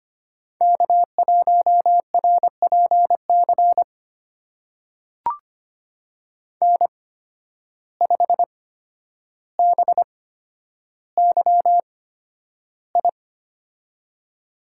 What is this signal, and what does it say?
Morse code, heard as a keyed tone, about 25 wpm, spaced as for 11 wpm, 700 Hz. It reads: K1RPC N5BYI
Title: USA Call Signs - 5 Characters - 8x Character Spacing - 25wpm